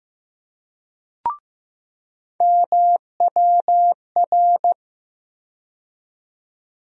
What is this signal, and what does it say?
Morse code, heard as a keyed tone, about 15 wpm, 700 Hz. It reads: MWR
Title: USA Call Sign Postfixes - 3 Letters 15wpm